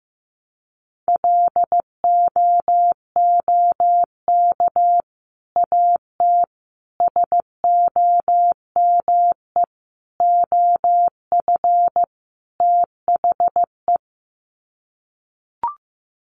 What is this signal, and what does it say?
Morse code, heard as a keyed tone, about 15 wpm, 700 Hz. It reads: LOOK AT SOME OF THE
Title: Sets of 5 Words Encore 15wpm